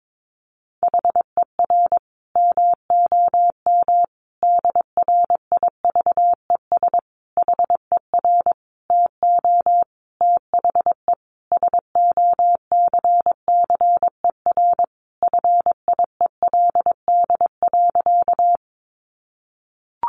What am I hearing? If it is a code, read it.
Morse code, heard as a keyed tone, about 22 wpm, 700 Hz. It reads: HER MOM DRIVES HER TO THE SOCCER FIELD.
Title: Easy Phrases 22wpm